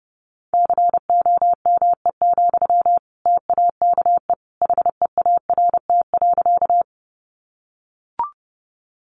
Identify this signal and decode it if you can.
Morse code, heard as a keyed tone, about 30 wpm, 700 Hz. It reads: COME, TAKE HEART.
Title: Sentences from Top 1200 Words 30wpm